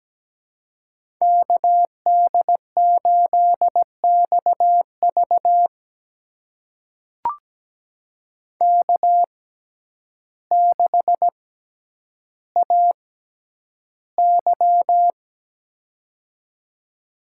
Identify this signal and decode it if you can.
Morse code, heard as a keyed tone, about 17 wpm, spaced as for 9 wpm, 700 Hz. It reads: KD8XV K6AY
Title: USA Call Signs - 6x Character Spacing - 17wpm